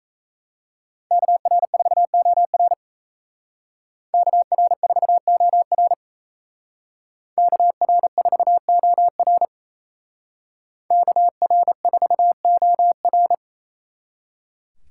Morse code, heard as a keyed tone, about 36 wpm, 700 Hz. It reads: KR4OR KR4OR KR4OR KR4OR